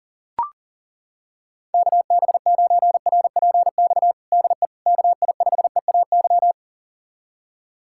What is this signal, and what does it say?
Morse code, heard as a keyed tone, about 40 wpm, 700 Hz. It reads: KB9RPX DE KI5EAY
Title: USA Callsign DE Callsign - 6 Characters - 40wpm